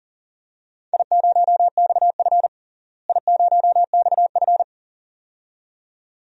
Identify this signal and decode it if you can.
Morse code, heard as a keyed tone, about 40 wpm, 700 Hz. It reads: I0XF I0XF
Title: DXCC-100 Call Signs - 4 Characters - Repeated 2x - 40wpm